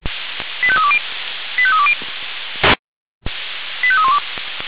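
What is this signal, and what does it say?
Five-tone selective call (ZVEI), heard as quick runs of five tones, tones of about 70 ms. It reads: 85320 85320 85312
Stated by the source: ZVEI five-tone selective call received on an SDR.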